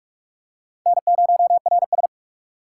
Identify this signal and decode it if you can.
Morse code, heard as a keyed tone, about 38 wpm, 700 Hz. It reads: N0RS